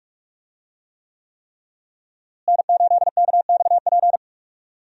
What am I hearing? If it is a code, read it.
Morse code, heard as a keyed tone, about 41 wpm, 700 Hz. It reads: N8KXP
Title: USA Call Signs - 5 Characters - Repeated 3x - 45wpm